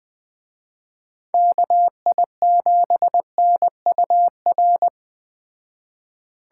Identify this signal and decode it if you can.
Morse code, heard as a keyed tone, about 20 wpm, 700 Hz. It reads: KI7NUR
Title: Speed Racing - USA 2x3 Call Signs 17wpm